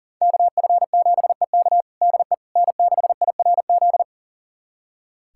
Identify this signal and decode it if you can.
Morse code, heard as a keyed tone, about 37 wpm, 700 Hz. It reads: KF7EK DE N6IRZ
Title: USA Callsign DE Callsign - 5 Characters - 40wpm